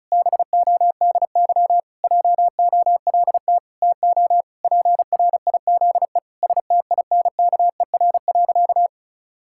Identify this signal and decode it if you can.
Morse code, heard as a keyed tone, about 36 wpm, 700 Hz. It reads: BODY JOLT TO PRIZE STINKER.
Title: Pangram Challenge v2 - 35wpm